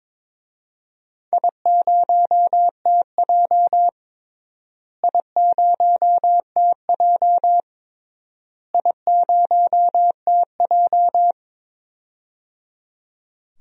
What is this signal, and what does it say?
Morse code, heard as a keyed tone, about 22 wpm, 700 Hz. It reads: I0TJ I0TJ I0TJ
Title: DXCC-100 Call Signs - 4 Characters - Repeated 3x - 22wpm